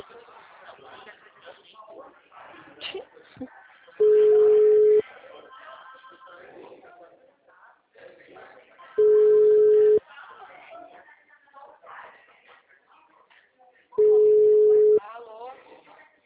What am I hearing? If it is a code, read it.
Ringback tone (Europe, 425 Hz), with speech.